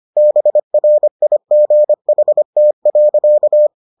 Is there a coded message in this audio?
Yes: Morse code from 0.2 to 3.7 s, heard as a keyed tone, about 25 wpm, 600 Hz. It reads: BRIGHT.